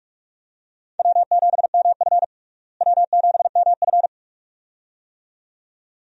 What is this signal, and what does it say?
Morse code, heard as a keyed tone, about 41 wpm, 700 Hz. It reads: W7MF W7MF
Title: USA Call Signs - 4 Characters - Repeated 2x - 45wpm